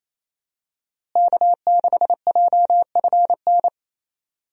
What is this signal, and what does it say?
Morse code, heard as a keyed tone, about 27 wpm, 700 Hz. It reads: K6JFN
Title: USA Call Signs - 5 Characters - Repeated 2x - 28wpm